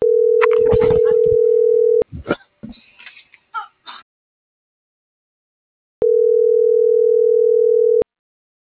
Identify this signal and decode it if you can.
Ringback tone (North America), 440 + 480 Hz.
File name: CENTRAL MABE / out miriam.soledad